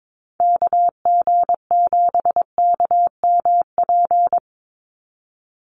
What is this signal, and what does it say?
Morse code, heard as a keyed tone, about 22 wpm, 700 Hz. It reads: KG7KMP
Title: USA Call Signs 22wpm